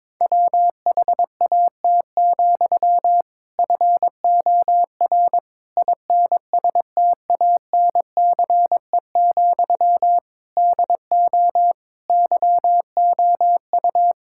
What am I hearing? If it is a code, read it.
Morse code, heard as a keyed tone, about 21 wpm, 700 Hz. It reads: WHAT, FOR INSTANCE, DO YOU
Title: Sentences from Top 1000 Words - Review Part 5 - 22wpm